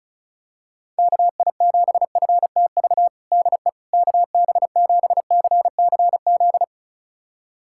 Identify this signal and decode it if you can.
Morse code, heard as a keyed tone, about 36 wpm, 700 Hz. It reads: KI7FTV DE KB7CCZ
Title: USA Callsign DE Callsign - 6 Characters - 35wpm